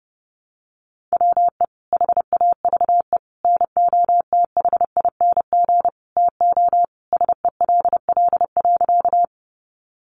Morse code, heard as a keyed tone, about 30 wpm, 700 Hz. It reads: WE HAVE NOTHING TO SELL.
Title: Sentences from Top 1100 Words 30wpm